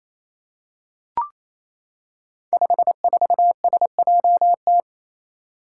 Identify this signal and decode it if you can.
Morse code, heard as a keyed tone, about 28 wpm, 700 Hz. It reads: 54SJT
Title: Five Challenging Letter-Number Combinations 28wpm